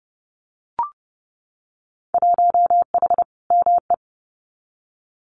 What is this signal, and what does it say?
Morse code, heard as a keyed tone, about 30 wpm, 700 Hz. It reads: 1H ME
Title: Winter Field Day Exchanges 30wpm